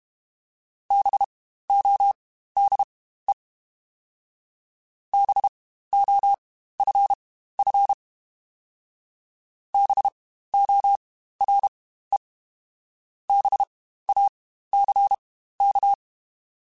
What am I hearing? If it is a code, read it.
Morse code, heard as a keyed tone, about 32 wpm, spaced as for 15 wpm, 800 Hz. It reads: BODE BOFF BORE BACK
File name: Spaced Alphas / Spaced Alphas mp3s / Bs sp @30